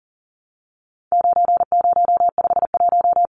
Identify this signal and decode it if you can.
Morse code, heard as a keyed tone, about 40 wpm, 700 Hz. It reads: 9051
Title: Numbers - 4-Digit 40wpm